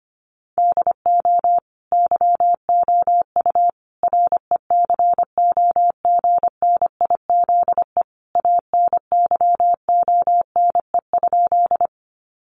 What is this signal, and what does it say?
Morse code, heard as a keyed tone, about 25 wpm, 700 Hz. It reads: DO YOU RECOGNIZE ANYONE?
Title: Sentences from Top 1400 Words 25wpm